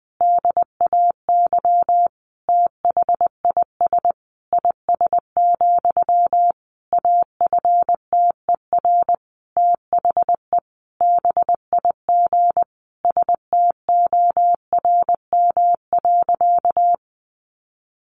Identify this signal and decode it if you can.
Morse code, heard as a keyed tone, about 20 wpm, 700 Hz. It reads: DAY THIS IS, AFTER THE BIG STORM.
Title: Sentences from Top 1500 Words 20wpm